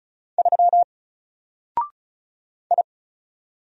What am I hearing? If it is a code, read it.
Morse code, heard as a keyed tone, about 35 wpm, 700 Hz. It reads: 3 I
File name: Single-Letter-Number-35wpm